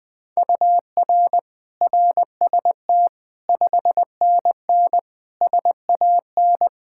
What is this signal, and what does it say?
Morse code, heard as a keyed tone, about 21 wpm, 700 Hz. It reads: UR RST 5NN SAN
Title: Practice QSO - Exchange 1 - Beginning - 20wpm